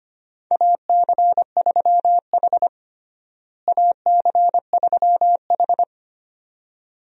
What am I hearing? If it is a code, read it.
Morse code, heard as a keyed tone, about 25 wpm, 700 Hz. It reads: AC3H AC3H
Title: USA Call Signs - 4 Characters - Repeated 2x - 25wpm